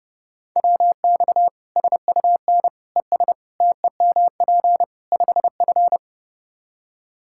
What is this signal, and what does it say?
Morse code, heard as a keyed tone, about 30 wpm, 700 Hz. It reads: WX SUN ES TEMP 5F
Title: Practice QSO - Exchange 2 - Info - 30wpm